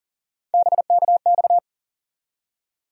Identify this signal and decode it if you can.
Morse code, heard as a keyed tone, about 40 wpm, 700 Hz. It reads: BKX
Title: USA Call Sign Postfixes - 3 Letters 40wpm